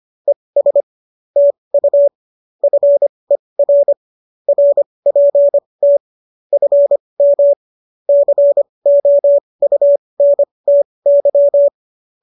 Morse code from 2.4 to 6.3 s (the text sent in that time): FER RPT